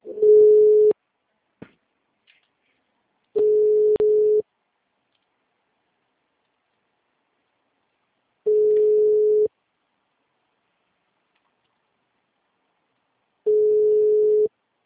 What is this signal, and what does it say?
Ringback tone (Europe, 425 Hz).
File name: out- -2005- - - .458